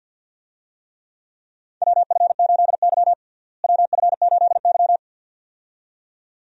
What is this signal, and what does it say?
Morse code, heard as a keyed tone, about 51 wpm, 700 Hz. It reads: WF8Y WF8Y
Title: USA Call Signs - 4 Characters - Repeated 2x - 50wpm